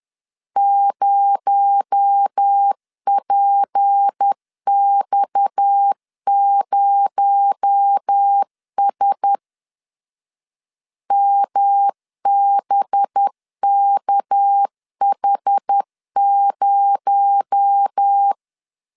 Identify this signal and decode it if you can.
Morse code, heard as a keyed tone, about 10 wpm, 800 Hz. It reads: 0PX0S MBKH0
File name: CW / DET / 1 9 1500C S A 10wpm 4511